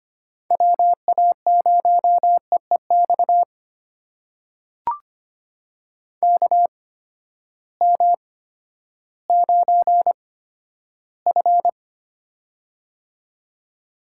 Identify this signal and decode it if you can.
Morse code, heard as a keyed tone, about 25 wpm, spaced as for 13 wpm, 700 Hz. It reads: WA0EEX KM9F
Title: USA Call Signs - 8x Character Spacing - 25wpm